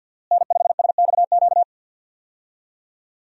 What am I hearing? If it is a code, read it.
Morse code, heard as a keyed tone, about 51 wpm, 700 Hz. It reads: N5SXQ
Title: USA Call Signs - 5 Characters - Repeated 2x - 50wpm